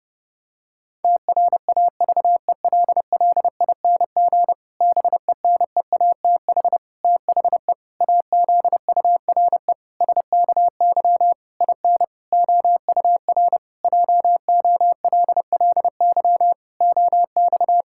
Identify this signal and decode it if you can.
Morse code, heard as a keyed tone, about 32 wpm, 700 Hz. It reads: TRAVELLING BENEATH THE AZURE SKY IN OUR JOLLY OX
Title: Pangram Challenge v2 - 30wpm